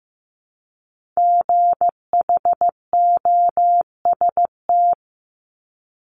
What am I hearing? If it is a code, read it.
Morse code, heard as a keyed tone, about 15 wpm, 700 Hz. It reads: GHOST
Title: Multi-letter Phonemes 15wpm